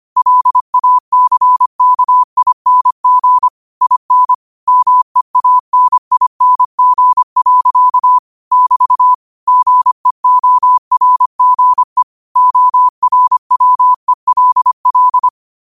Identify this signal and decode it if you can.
Morse code, heard as a keyed tone, about 25 wpm, 1000 Hz. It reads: LACKING IN MEANING. <BT> GEORGE ORWELL
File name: qotdfiles / 25wpm